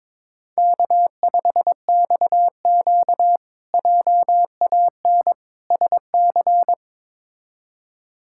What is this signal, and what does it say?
Morse code, heard as a keyed tone, about 22 wpm, 700 Hz. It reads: K5XQ JAN SC
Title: K1USN SST QSO Practice - 22wpm